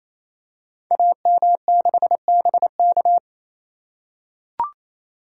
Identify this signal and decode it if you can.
Morse code, heard as a keyed tone, about 28 wpm, 700 Hz. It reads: AM6BK
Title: Code Groups - 5 Letter:Numbers - 28wpm